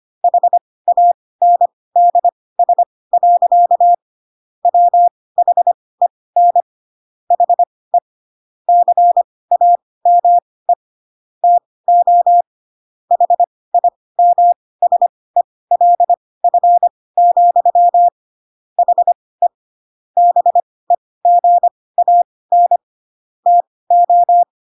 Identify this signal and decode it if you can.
Morse code, heard as a keyed tone, about 25 wpm, spaced as for 18 wpm, 700 Hz. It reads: HANDS. WHEN HE CAME TO HIMSELF, HE BEGAN TO